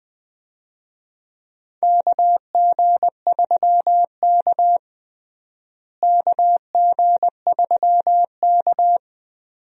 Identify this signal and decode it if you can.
Morse code, heard as a keyed tone, about 21 wpm, 700 Hz. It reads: KG3K KG3K
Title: USA Call Signs - 4 Characters - Repeated 2x - 20wpm